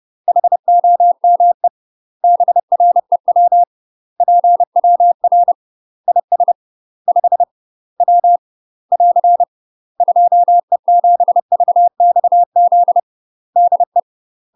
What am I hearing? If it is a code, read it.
Morse code, heard as a keyed tone, about 30 wpm, 700 Hz. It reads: HOME BREW PWR IS 5 W <AR> 2E7VXZ DE
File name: UK MASTER QSOs TRAINING / 30WPM / UK MASTER QSOs 0027